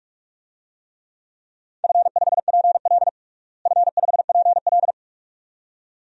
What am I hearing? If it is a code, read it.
Morse code, heard as a keyed tone, about 46 wpm, 700 Hz. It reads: F5PL F5PL
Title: DXCC-100 Call Signs - 4 Characters - Repeated 2x - 45wpm